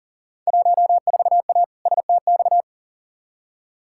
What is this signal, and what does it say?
Morse code, heard as a keyed tone, about 40 wpm, 700 Hz. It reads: 14A STX
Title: Field Day Exchanges Encore - 40wpm